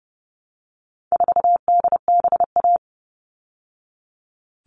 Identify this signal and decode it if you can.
Morse code, heard as a keyed tone, about 30 wpm, 700 Hz. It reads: 4DBA
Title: All Cut Numbers - 4-Digit 30wpm